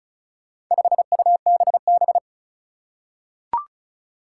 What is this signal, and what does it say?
Morse code, heard as a keyed tone, about 36 wpm, 700 Hz. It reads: 5UBB
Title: Four Challenging Letter-Number Combinations 35wpm